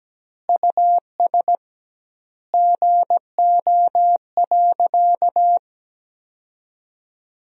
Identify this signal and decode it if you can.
Morse code, heard as a keyed tone, about 17 wpm, 700 Hz. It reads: US GO.
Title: 2x Word-Spacing - Sentences from Top 400 Words - 17wpm